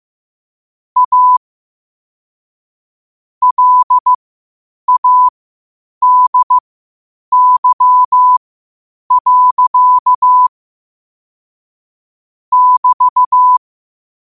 Morse code, heard as a keyed tone, about 15 wpm, spaced as for 8 wpm, 1000 Hz. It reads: A LADY. <BT>
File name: qotdfiles / 8wpm